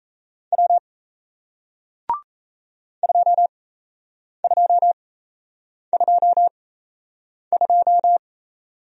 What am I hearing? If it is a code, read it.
Morse code, heard as a keyed tone, about 36 wpm, spaced as for 8 wpm, 700 Hz. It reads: W 2222